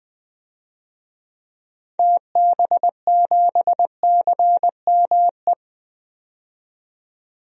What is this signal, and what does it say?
Morse code, heard as a keyed tone, about 20 wpm, 700 Hz. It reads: TB7CME
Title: DXCC-100 Call Signs - 6 Characters - 20wpm